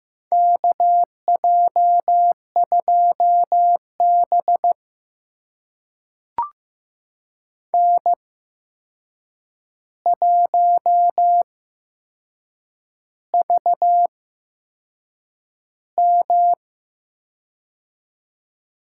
Morse code, heard as a keyed tone, about 15 wpm, spaced as for 6 wpm, 700 Hz. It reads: KJ2B N1VM
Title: USA Call Signs - 4 Characters - 8x Character Spacing - 15wpm